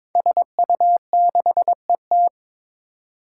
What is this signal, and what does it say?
Morse code, heard as a keyed tone, about 22 wpm, 700 Hz. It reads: SU6ET
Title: Five Challenging Letter-Number Combinations 22wpm